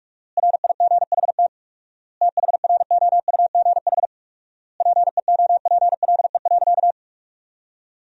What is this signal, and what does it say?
Morse code, heard as a keyed tone, about 46 wpm, 700 Hz. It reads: RIGHT THROUGH PEOPLE.
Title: 4x Word-Spacing - Sentences from Top 200 Words - 45wpm